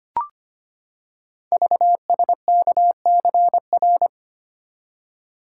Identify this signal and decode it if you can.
Morse code, heard as a keyed tone, about 25 wpm, 700 Hz. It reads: VSKCR
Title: Code Groups - 5 Letters - 25wpm